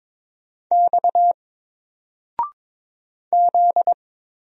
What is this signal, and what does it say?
Morse code, heard as a keyed tone, about 22 wpm, 700 Hz. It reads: X Z
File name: Single-Letter-Number-22wpm